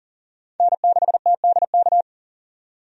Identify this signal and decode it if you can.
Morse code, heard as a keyed tone, about 40 wpm, 700 Hz. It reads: N6TDK
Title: USA Call Signs - 5 Characters - 40wpm